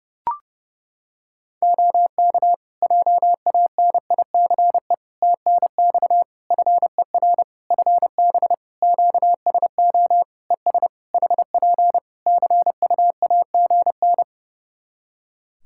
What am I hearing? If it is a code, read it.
Morse code, heard as a keyed tone, about 30 wpm, 700 Hz. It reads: OK JANICE TNX FER FB QSO ES HP CUAGN
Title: K1USN SST QSO Practice 30wpm